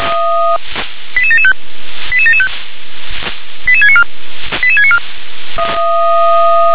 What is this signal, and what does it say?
Five-tone selective call (ZVEI), heard as quick runs of five tones, tones of about 70 ms. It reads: 88775 88775 88694 88694
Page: /357/SDR/ZVEI/88791/